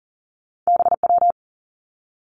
Morse code, heard as a keyed tone, about 42 wpm, 700 Hz. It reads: B W
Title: Two Challenging Letter-Number Combinations 40wpm